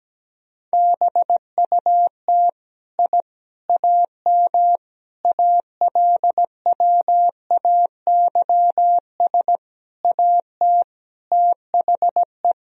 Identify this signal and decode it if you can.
Morse code, heard as a keyed tone, about 17 wpm, 700 Hz. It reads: BUT I AM ALWAYS AT THE
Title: Sentences from Top 1000 Words - Review Part 2 - 17wpm